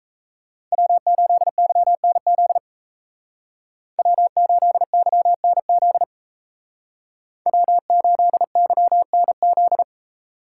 Morse code, heard as a keyed tone, about 37 wpm, 700 Hz. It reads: W8YNZ W8YNZ W8YNZ